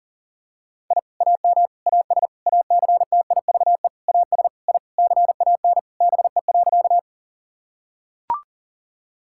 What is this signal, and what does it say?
Morse code, heard as a keyed tone, about 40 wpm, 700 Hz. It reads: I AM AS ACTIVE AS I CAN BE.
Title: Sentences from Top 1000 Words - Review Part 6 - 40wpm